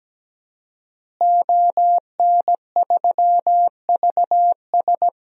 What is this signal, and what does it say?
Morse code, heard as a keyed tone, about 17 wpm, 700 Hz. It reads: ON3VS
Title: DXCC-100 Call Signs - 5 Characters - Repeated 3x - 17wpm